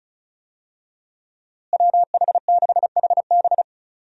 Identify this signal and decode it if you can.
Morse code, heard as a keyed tone, about 34 wpm, 700 Hz. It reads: WH6HB
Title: USA Call Signs - 5 Characters - Repeated 3x - 35wpm